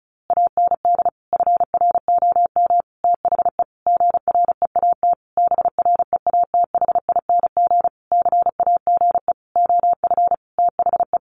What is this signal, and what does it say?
Morse code, heard as a keyed tone, about 34 wpm, 700 Hz. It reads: AND FROM THE GREAT BREATHING CAGE OF THE
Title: A Sound Of Thunder 35wpm